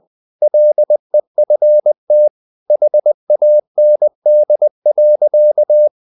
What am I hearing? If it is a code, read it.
Morse code, heard as a keyed tone, about 20 wpm, 600 Hz. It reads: LEFT HAND.
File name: the old man and the sea part 2 20wpm